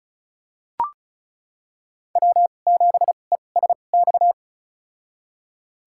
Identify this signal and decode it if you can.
Morse code, heard as a keyed tone, about 36 wpm, spaced as for 25 wpm, 700 Hz. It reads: W7ESX
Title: USA Call Signs - 5 Characters - 2x Character Spacing - 35wpm